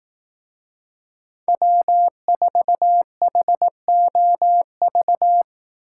Morse code, heard as a keyed tone, about 20 wpm, 700 Hz. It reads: W4HOV